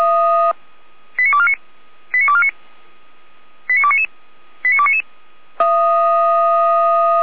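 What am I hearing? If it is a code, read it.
Five-tone selective call (ZVEI), heard as quick runs of five tones, tones of about 80 ms. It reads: 89249 89249 89299 89299
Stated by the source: ZVEI five-tone selective call 89501 received on SDR, 455 channel.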